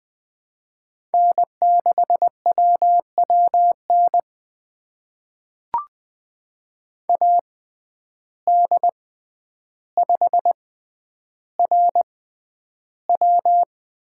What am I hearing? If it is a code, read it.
Morse code, heard as a keyed tone, about 20 wpm, spaced as for 10 wpm, 700 Hz. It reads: N6WWN AD5RW
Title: USA Call Signs - 5 Characters - 6x Character Spacing - 20wpm